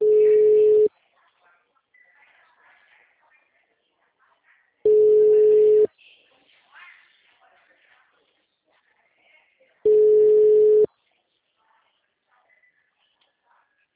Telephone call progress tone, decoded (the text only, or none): ringback tone